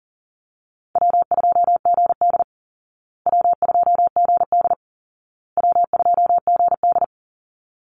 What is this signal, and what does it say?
Morse code, heard as a keyed tone, about 40 wpm, 700 Hz. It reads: W2GD W2GD W2GD
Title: Easy Call Signs Repeated 3x - 40wpm